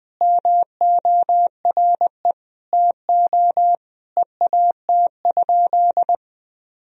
Morse code, heard as a keyed tone, about 20 wpm, 700 Hz. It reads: MORE TO EAT?
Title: Sentences from Top 300 Words and Abbreviations 20wpm